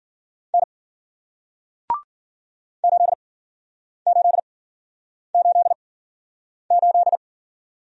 Morse code, heard as a keyed tone, about 54 wpm, spaced as for 9 wpm, 700 Hz. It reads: N 8888